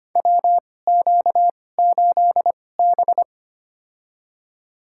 Morse code, heard as a keyed tone, about 25 wpm, spaced as for 18 wpm, 700 Hz. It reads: WQ8B
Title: USA Call Signs - 2x Character Spacing - 25wpm